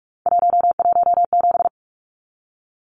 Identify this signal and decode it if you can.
Morse code, heard as a keyed tone, about 44 wpm, 700 Hz. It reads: 117
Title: Numbers - 3-Digit 45wpm